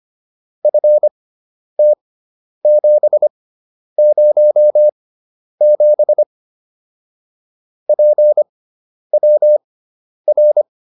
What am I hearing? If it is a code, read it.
Morse code, heard as a keyed tone, about 25 wpm, spaced as for 10 wpm, 600 Hz. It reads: FT707 PWR